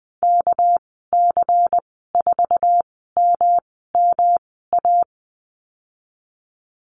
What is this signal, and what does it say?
Morse code, heard as a keyed tone, about 20 wpm, spaced as for 15 wpm, 700 Hz. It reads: KC4MMA
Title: 213 Lesson 34g - Number 8 - Call Signs 20wpm